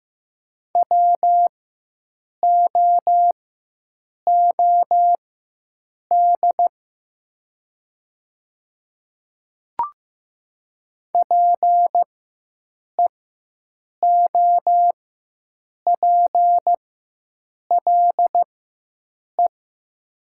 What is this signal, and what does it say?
Morse code, heard as a keyed tone, about 15 wpm, spaced as for 7 wpm, 700 Hz. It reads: WOOD PEOPLE
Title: 4x Character Spacing - Top 300 Words - 15wpm